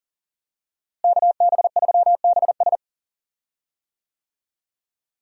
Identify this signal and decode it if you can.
Morse code, heard as a keyed tone, about 37 wpm, 700 Hz. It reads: KB3BS
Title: USA Call Signs - 5 Characters - Repeated 3x - 40wpm